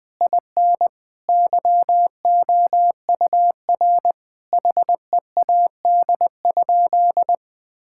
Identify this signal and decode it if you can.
Morse code, heard as a keyed tone, about 20 wpm, 700 Hz. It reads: IN YOUR HEAD?
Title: Sentences from Top 1000 Words - Review Part 5 - 20wpm